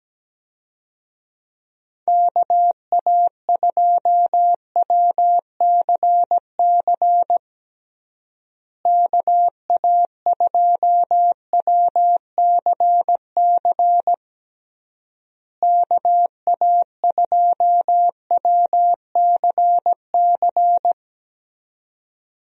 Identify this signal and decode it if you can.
Morse code, heard as a keyed tone, about 17 wpm, 700 Hz. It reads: KA2WCC KA2WCC KA2WCC
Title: USA Call Signs - 6 Characters - Repeated 3x - 17wpm